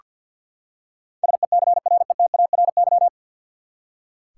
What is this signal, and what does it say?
Morse code, heard as a keyed tone, about 46 wpm, 700 Hz. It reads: SECRETARY
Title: K1USN SST QSO Practice 50wpm